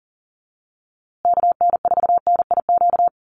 Morse code, heard as a keyed tone, about 40 wpm, 700 Hz. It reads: KN4NIQ
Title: USA Call Signs Repeated 3x 40wpm